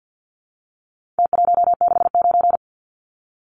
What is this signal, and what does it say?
Morse code, heard as a keyed tone, about 50 wpm, 700 Hz. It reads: T169
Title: Numbers - Contest Style - 4-Digit 50wpm